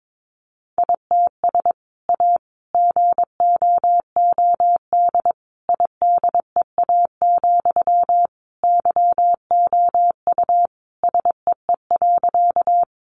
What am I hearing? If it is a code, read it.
Morse code, heard as a keyed tone, about 22 wpm, 700 Hz. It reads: ITS A GOOD IDEA, YOU SEE.
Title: Sentences from Top 600 Words 22wpm